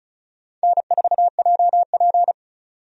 Morse code, heard as a keyed tone, about 36 wpm, 700 Hz. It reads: N4JP
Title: USA Call Signs - 4 Characters - 35wpm